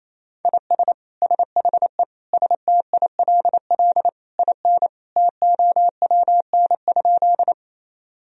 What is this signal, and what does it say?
Morse code, heard as a keyed tone, about 28 wpm, 700 Hz. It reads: IS SHE STILL IN TOWN?
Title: Sentences from Top 1000 Words - Review Part 2 - 28wpm